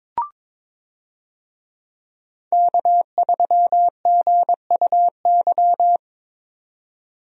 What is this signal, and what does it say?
Morse code, heard as a keyed tone, about 22 wpm, 700 Hz. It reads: K3GUY
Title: USA Call Signs - 5 Characters - 22wpm